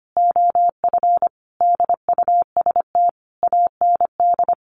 Morse code, heard as a keyed tone, about 25 wpm, 700 Hz. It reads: OF DUST AND
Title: A Sound Of Thunder 25wpm